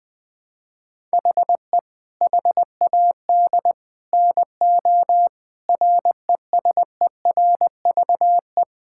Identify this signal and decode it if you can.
Morse code, heard as a keyed tone, about 20 wpm, 700 Hz. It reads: HE HAD NO RESERVE
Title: Sentences from Top 1000 Words - Review Part 6 - 20wpm